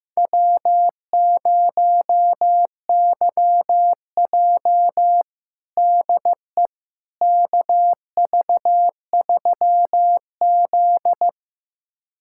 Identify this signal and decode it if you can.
Morse code, heard as a keyed tone, about 15 wpm, 700 Hz. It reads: W0YJ DE KV3Z
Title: USA Callsign DE Callsign - 4 Characters - 15wpm